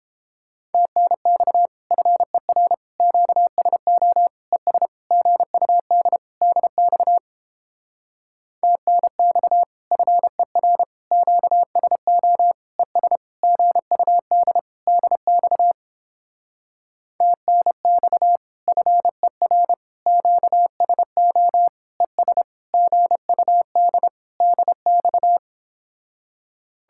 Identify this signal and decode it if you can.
Morse code, heard as a keyed tone, about 29 wpm, 700 Hz. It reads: TNX FER QSO ES GUD DX TNX FER QSO ES GUD DX TNX FER QSO ES GUD DX